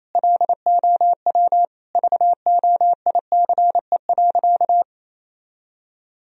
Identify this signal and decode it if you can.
Morse code, heard as a keyed tone, about 28 wpm, 700 Hz. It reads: LOW VOICE.
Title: Sentences from Top 1000 Words - Review Part 1 - 28wpm